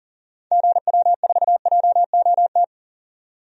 Morse code, heard as a keyed tone, about 38 wpm, 700 Hz. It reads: GW4JOT
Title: DXCC-100 Call Signs - 6 Characters - 40wpm